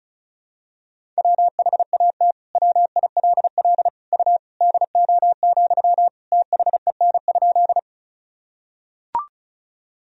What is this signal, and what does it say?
Morse code, heard as a keyed tone, about 36 wpm, 700 Hz. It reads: WHAT WILL U DO, THEN?
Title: Sentences from Top 100 Words and Abbreviations 35wpm